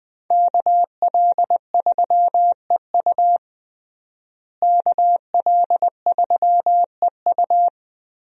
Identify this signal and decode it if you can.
Morse code, heard as a keyed tone, about 21 wpm, 700 Hz. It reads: KL3EU KL3EU
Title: USA Call Signs - 5 Characters - Repeated 3x - 20wpm